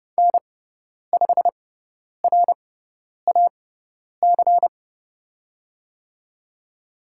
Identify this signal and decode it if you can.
Morse code, heard as a keyed tone, about 30 wpm, spaced as for 10 wpm, 700 Hz. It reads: N5RAC
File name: Lesson 16g - Letter c - Call Signs 30wpm